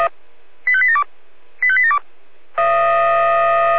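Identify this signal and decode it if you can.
Five-tone selective call (ZVEI), heard as quick runs of five tones, tones of about 70 ms. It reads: 85872 85872
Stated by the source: Five-tone ZVEI selective call received with an SDR.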